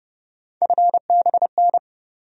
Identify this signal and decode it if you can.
Morse code, heard as a keyed tone, about 30 wpm, 700 Hz. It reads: FBN
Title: USA Call Sign Postfixes - 3 Letters 30wpm